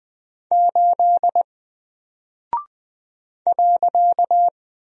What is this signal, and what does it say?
Morse code, heard as a keyed tone, about 20 wpm, 700 Hz. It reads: 8 .
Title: Single Character (Letter:Number:Punctuation:Prosign) Alt 20wpm